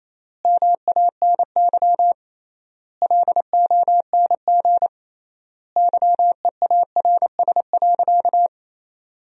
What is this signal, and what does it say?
Morse code, heard as a keyed tone, about 28 wpm, 700 Hz. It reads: MANY LONG YEARS.
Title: 3x Word-Spacing - Sentences from Top 200 Words - 28wpm